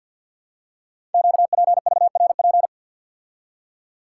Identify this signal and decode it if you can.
Morse code, heard as a keyed tone, about 50 wpm, 700 Hz. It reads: QPVRP
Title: Code Groups - 5 Letters - 50wpm